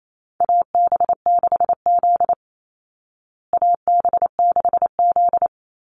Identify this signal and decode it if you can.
Morse code, heard as a keyed tone, about 28 wpm, 700 Hz. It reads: AB6Z AB6Z
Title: Call Signs Repeated 2x - 28wpm